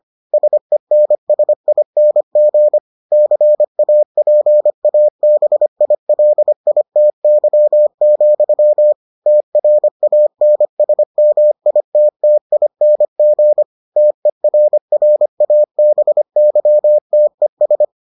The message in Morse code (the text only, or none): SENSING CAPABILITY, TRANSMITTING TERRABYTES